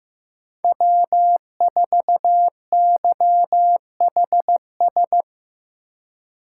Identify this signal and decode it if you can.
Morse code, heard as a keyed tone, about 16 wpm, 700 Hz. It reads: W4YHS
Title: USA Call Signs - 2x Character Spacing - 15wpm